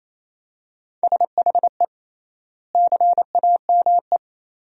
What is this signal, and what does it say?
Morse code, heard as a keyed tone, about 28 wpm, 700 Hz. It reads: SHE CAME
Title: 3x Word-Spacing - Sentences from Top 300 Words - 28wpm